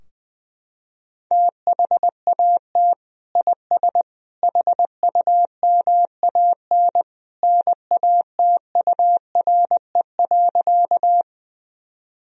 Morse code, heard as a keyed tone, about 20 wpm, 700 Hz. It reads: THAT IS HUMAN NATURE.
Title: Sentences from Top 1000 Words - Review Part 3 - 20wpm